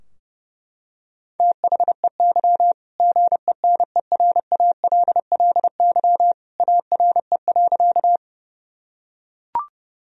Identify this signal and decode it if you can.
Morse code, heard as a keyed tone, about 30 wpm, 700 Hz. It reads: THEY GENERALLY ARE.
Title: Sentences from Top 1000 Words - Review Part 4 - 30wpm